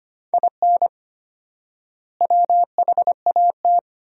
Morse code, heard as a keyed tone, about 25 wpm, 700 Hz. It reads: IN WHAT
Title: 4x Word-Spacing - Sentences from Top 200 Words - 25wpm